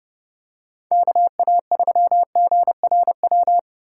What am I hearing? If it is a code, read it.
Morse code, heard as a keyed tone, about 30 wpm, 700 Hz. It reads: KA3GRW